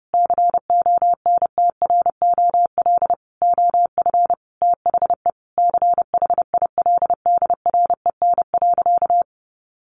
Morse code, heard as a keyed tone, about 30 wpm, 700 Hz. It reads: CONTROL OF THE CHILDREN.
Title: Sentences from Top 1200 Words 30wpm